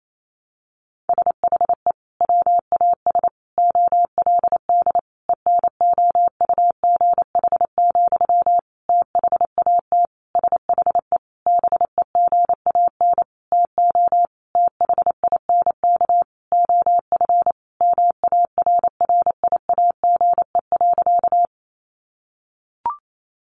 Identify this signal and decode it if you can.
Morse code, heard as a keyed tone, about 28 wpm, 700 Hz. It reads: SHE WAS OLD ENOUGH, THAT SHE BEGAN TO THINK OF MARRIAGE.
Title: Sentences from Top 800 Words 28wpm